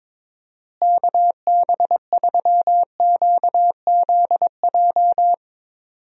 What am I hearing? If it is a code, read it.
Morse code, heard as a keyed tone, about 22 wpm, 700 Hz. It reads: KB3QZJ
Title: USA Call Signs - 6 Characters - Repeated 3x - 22wpm